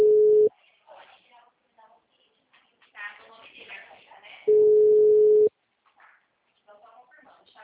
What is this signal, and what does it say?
Ringback tone (Europe, 425 Hz), with speech.